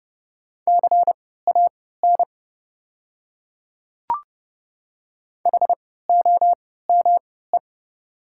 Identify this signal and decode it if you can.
Morse code, heard as a keyed tone, about 30 wpm, spaced as for 17 wpm, 690 Hz. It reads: CAN HOME
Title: 3x Character Spacing - Top 200 Words - 30wpm